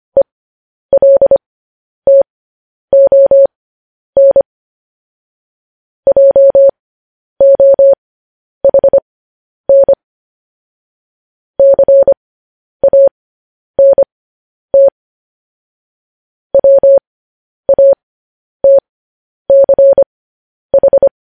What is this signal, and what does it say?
Morse code, heard as a keyed tone, about 25 wpm, spaced as for 10 wpm, 550 Hz. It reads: ELTON JOHN CANT WATCH
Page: news headlines in morse code at 10 wpm